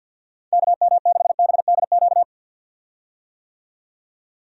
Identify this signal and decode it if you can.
Morse code, heard as a keyed tone, about 48 wpm, 700 Hz. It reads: KM6BDQ
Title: USA Call Signs - 6 Characters - 50wpm